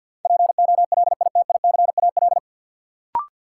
Morse code, heard as a keyed tone, about 53 wpm, 700 Hz. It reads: POLITICAL
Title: Sentences from Top 800 Words - New Word Drill - 50wpm